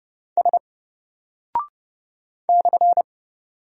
Morse code, heard as a keyed tone, about 30 wpm, 700 Hz. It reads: S /
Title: Single Character (Letter:Number:Punctuation:Prosign) 30wpm